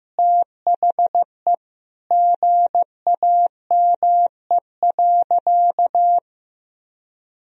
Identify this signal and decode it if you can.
Morse code, heard as a keyed tone, about 16 wpm, 700 Hz. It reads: THE GAME.
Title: Sentences from Top 1000 Words - Review Part 3 - 15wpm